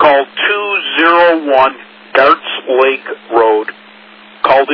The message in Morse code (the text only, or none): none